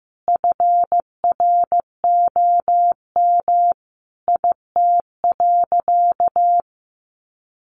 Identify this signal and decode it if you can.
Morse code, heard as a keyed tone, about 15 wpm, 700 Hz. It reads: FROM IT.
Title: Sentences from Top 800 Words 15wpm